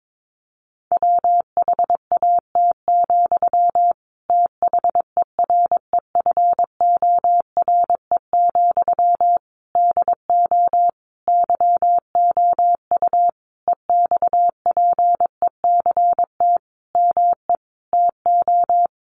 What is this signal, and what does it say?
Morse code, heard as a keyed tone, about 22 wpm, 700 Hz. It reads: WHAT, THEREFORE, DO YOU EXPECT ME TO
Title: Sentences from Top 1100 Words 22wpm